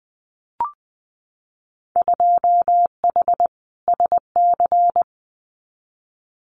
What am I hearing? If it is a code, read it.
Morse code, heard as a keyed tone, about 20 wpm, 700 Hz. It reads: 2H SC
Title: Winter Field Day Exchanges 20wpm